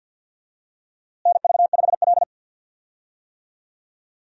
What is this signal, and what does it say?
Morse code, heard as a keyed tone, about 52 wpm, 700 Hz. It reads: NV5L
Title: USA Call Signs - 4 Characters - Repeated 3x - 50wpm